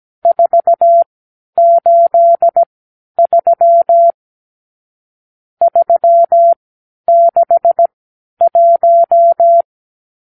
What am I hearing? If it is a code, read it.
Morse code, heard as a keyed tone, about 17 wpm, spaced as for 10 wpm, 680 Hz. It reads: 483 361
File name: arrl / Review (1-0)